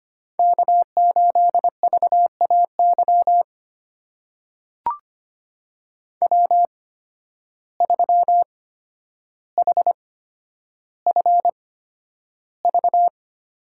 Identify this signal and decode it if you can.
Morse code, heard as a keyed tone, about 25 wpm, spaced as for 11 wpm, 700 Hz. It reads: K8VAY W3HFV
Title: USA Call Signs - 5 Characters - 8x Character Spacing - 25wpm